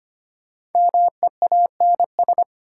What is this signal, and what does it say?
Morse code, heard as a keyed tone, about 25 wpm, 700 Hz. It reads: MEANS
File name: 5 Letter Words - First 3 Letters Given - 25wpm